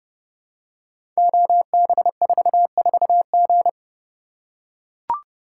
Morse code, heard as a keyed tone, about 30 wpm, 700 Hz. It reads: OB44G
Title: Code Groups - 5 Letter:Numbers - 30wpm